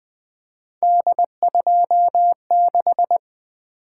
Morse code, heard as a keyed tone, about 20 wpm, 700 Hz. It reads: D26
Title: Three Letter-Number Combinations - 20wpm